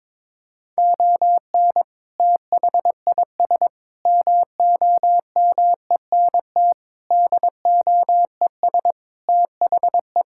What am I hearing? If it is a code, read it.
Morse code, heard as a keyed tone, about 22 wpm, 700 Hz. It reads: ON THIS MOMENT DOES THE
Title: Sentences from Top 1600 Words 22wpm